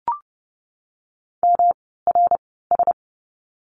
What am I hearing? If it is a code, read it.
Morse code, heard as a keyed tone, about 30 wpm, spaced as for 17 wpm, 700 Hz. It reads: MRS
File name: course / 30:10wpm / 108 Lesson 18f - Letter m - Words 30wpm